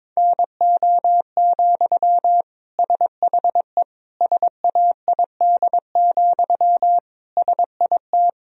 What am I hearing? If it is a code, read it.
Morse code, heard as a keyed tone, about 21 wpm, 700 Hz. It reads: NO, SHE SAID, SIT
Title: Sentences from Top 1000 Words - Review Part 1 - 22wpm